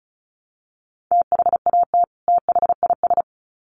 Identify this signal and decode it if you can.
Morse code, heard as a keyed tone, about 34 wpm, 700 Hz. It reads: THAT THIS
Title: Sets of 2 Words 35wpm